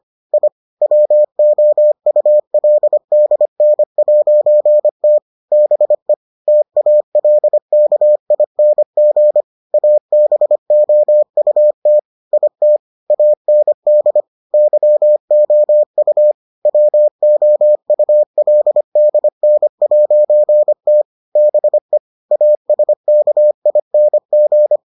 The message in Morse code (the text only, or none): I WOULDN'T BE TALKING ABOUT IT AND YOU WOULDN'T BE ASKING